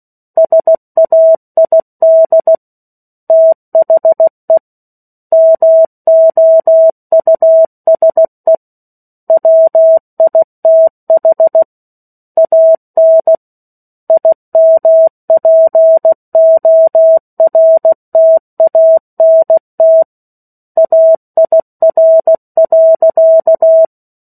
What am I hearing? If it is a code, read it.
Morse code, heard as a keyed tone, about 16 wpm, 650 Hz. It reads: SAID THE MOUSE WITH AN IMPORTANT AIR.
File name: 03-Chapter-03 16 WPM